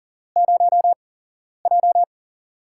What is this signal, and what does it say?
Morse code, heard as a keyed tone, about 40 wpm, spaced as for 11 wpm, 700 Hz. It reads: 0J
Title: USA Call Signs - 8x Character Spacing - 40wpm